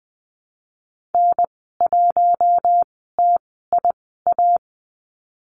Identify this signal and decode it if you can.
Morse code, heard as a keyed tone, about 20 wpm, spaced as for 15 wpm, 700 Hz. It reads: N1TIA
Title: 055 Lesson 10g - Letter r - Call Signs 20wpm